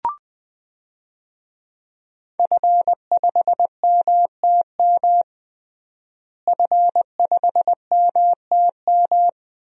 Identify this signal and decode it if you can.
Morse code, heard as a keyed tone, about 20 wpm, 700 Hz. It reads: F5MTM F5MTM
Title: DXCC-100 Call Signs - 5 Characters - Repeated 3x - 20wpm